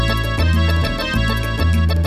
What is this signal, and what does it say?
There is no code.